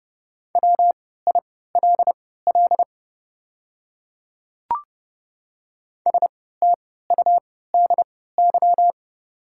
Morse code, heard as a keyed tone, about 30 wpm, spaced as for 17 wpm, 700 Hz. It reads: WILL STUDY
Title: 3x Character Spacing - Top 500 Words - 30wpm